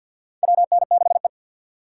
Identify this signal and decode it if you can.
Morse code, heard as a keyed tone, about 53 wpm, 700 Hz. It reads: WN6E
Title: USA Call Signs - 4 Characters - 3x Character Spacing - 50wpm